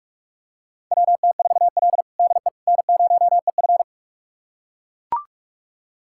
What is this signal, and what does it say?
Morse code, heard as a keyed tone, about 42 wpm, 700 Hz. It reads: WT4L DE N0EF
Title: USA Callsign DE Callsign - 4 Characters - 45wpm